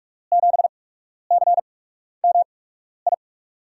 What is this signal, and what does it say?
Morse code, heard as a keyed tone, about 44 wpm, spaced as for 12 wpm, 700 Hz. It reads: 7CMI
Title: USA Call Signs - 6 Characters - 8x Character Spacing - 45wpm